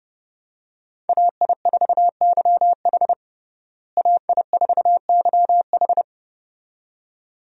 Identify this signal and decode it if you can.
Morse code, heard as a keyed tone, about 30 wpm, 700 Hz. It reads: AI4YH AI4YH
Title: USA Call Signs - 5 Characters - Repeated 2x - 30wpm